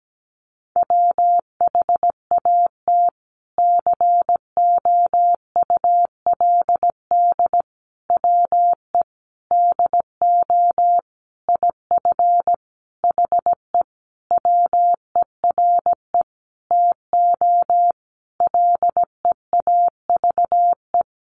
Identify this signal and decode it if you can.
Morse code, heard as a keyed tone, about 17 wpm, 700 Hz. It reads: WHAT COULD WE DO IF HE WERE TO LEAVE
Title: Sentences from Top 400 Words 17wpm